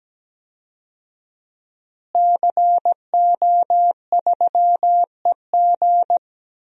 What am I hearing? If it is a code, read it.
Morse code, heard as a keyed tone, about 17 wpm, 700 Hz. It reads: CO3EG
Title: DXCC-100 Call Signs - 5 Characters - Repeated 2x - 17wpm